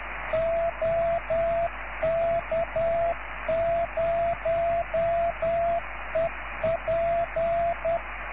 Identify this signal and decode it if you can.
Morse code, heard as a keyed tone, about 10 wpm, 680 Hz, with radio noise. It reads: OK0EP